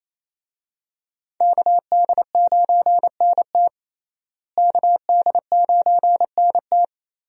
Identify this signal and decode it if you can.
Morse code, heard as a keyed tone, about 27 wpm, 700 Hz. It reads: KD9NT KD9NT
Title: USA Call Signs - 5 Characters - Repeated 2x - 28wpm